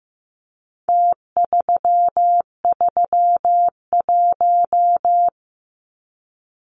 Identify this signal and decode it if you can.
Morse code, heard as a keyed tone, about 15 wpm, 700 Hz. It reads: T331
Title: Numbers - Contest Style - 4-Digit 15wpm